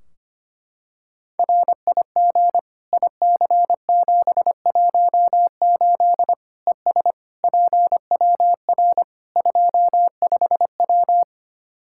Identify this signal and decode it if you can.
Morse code, heard as a keyed tone, about 25 wpm, 700 Hz. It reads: RIG IC718 ES PWR 25W
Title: Practice QSO - Exchange 2 - Info - 25wpm